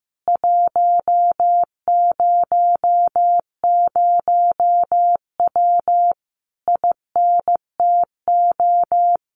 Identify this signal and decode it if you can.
Morse code, heard as a keyed tone, about 15 wpm, 700 Hz. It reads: 100W INTO
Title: Common QSO Phrases Encore 15wpm